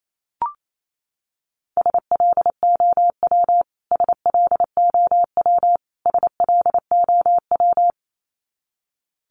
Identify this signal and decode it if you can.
Morse code, heard as a keyed tone, about 28 wpm, 700 Hz. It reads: SLOW SLOW SLOW
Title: Top 500 Words Repeated 28wpm